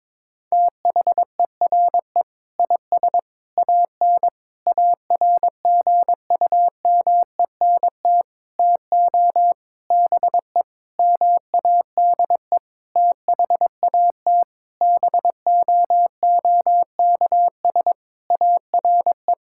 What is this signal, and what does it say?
Morse code, heard as a keyed tone, about 22 wpm, 700 Hz. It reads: THERE IS AN ARGUMENT TO BE MADE THAT BOOKS ARE
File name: practice / 22wpm / Sentences with 8-Word Phrases - 22wpm